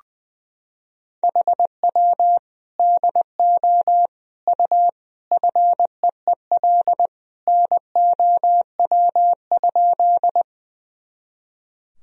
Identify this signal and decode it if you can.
Morse code, heard as a keyed tone, about 20 wpm, 700 Hz. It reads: HW DO U FEEL NOW?
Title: Sentences from Top 400 Words and Abbreviations 20wpm